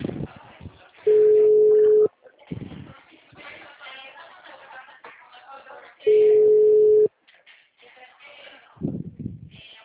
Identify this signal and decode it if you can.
Ringback tone (Europe, 425 Hz), with speech.